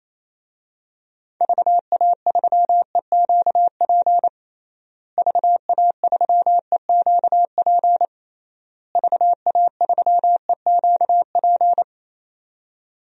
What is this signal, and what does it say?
Morse code, heard as a keyed tone, about 28 wpm, 700 Hz. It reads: VA3EQP VA3EQP VA3EQP
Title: DXCC-100 Call Signs - 6 Characters - Repeated 3x - 28wpm